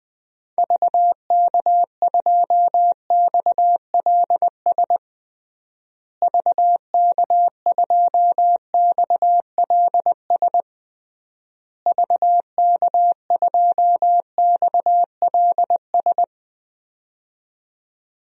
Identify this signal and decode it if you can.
Morse code, heard as a keyed tone, about 20 wpm, 700 Hz. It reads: VK2XLS VK2XLS VK2XLS
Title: DXCC-100 Call Signs - 6 Characters - Repeated 3x - 20wpm